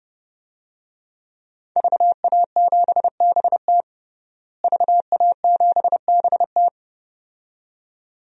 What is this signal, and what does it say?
Morse code, heard as a keyed tone, about 32 wpm, 700 Hz. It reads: VA7BT VA7BT
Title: DXCC-100 Call Signs - 5 Characters - Repeated 2x - 30wpm